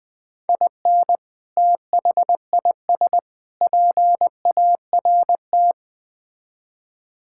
Morse code, heard as a keyed tone, about 21 wpm, 700 Hz. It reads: IN THIS PART
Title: Build A Phrase - 5-Word Phrases - 20wpm